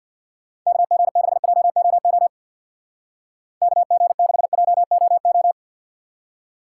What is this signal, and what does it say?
Morse code, heard as a keyed tone, about 58 wpm, 700 Hz. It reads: KG6JOO KG6JOO
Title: Speed Racing - USA 2x3 Call Signs 50wpm